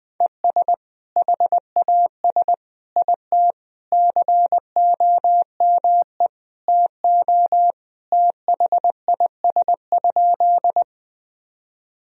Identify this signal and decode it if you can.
Morse code, heard as a keyed tone, about 21 wpm, 700 Hz. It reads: ES HAS IT COME TO THIS?
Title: Sentences from Top 100 Words and Abbreviations 20wpm